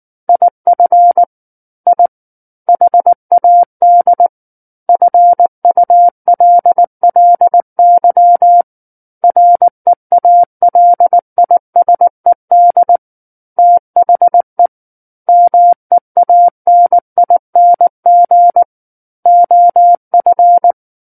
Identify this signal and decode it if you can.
Morse code, heard as a keyed tone, about 19 wpm, 700 Hz. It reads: IF I HAD FULLY REALISED THE MEANING OF
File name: Book One / 10-Chapter 10 - 19 WPM